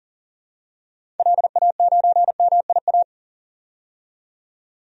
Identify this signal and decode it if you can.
Morse code, heard as a keyed tone, about 40 wpm, 700 Hz. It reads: LA9MIA
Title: DXCC-100 Call Signs - 6 Characters - 40wpm